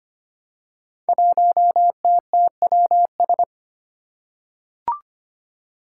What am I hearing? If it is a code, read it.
Morse code, heard as a keyed tone, about 25 wpm, 700 Hz. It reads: 1TTWS
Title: Five Challenging Letter-Number Combinations 25wpm